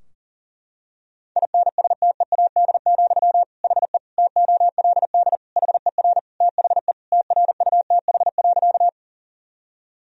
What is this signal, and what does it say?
Morse code, heard as a keyed tone, about 40 wpm, 700 Hz. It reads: INSTEAD, HE TOLD HER THE TRUTH.
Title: Sentences from Top 1000 Words - Review Part 4 - 40wpm